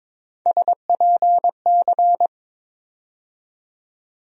Morse code, heard as a keyed tone, about 22 wpm, 700 Hz. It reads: SPC
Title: USA Call Sign Postfixes - 3 Letters 22wpm